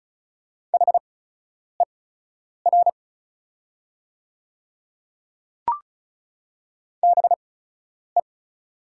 Morse code, heard as a keyed tone, about 36 wpm, spaced as for 10 wpm, 700 Hz. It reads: HER BE
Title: 8x Character Spacing - Top 100 Words - 35wpm